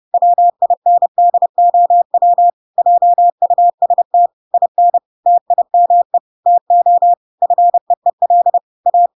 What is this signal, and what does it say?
Morse code, heard as a keyed tone, about 30 wpm, 700 Hz. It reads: WINDOW JUST IN TIME TO FEEL A
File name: Pinocchio / Pinocchio ch6 30 30 ik1hge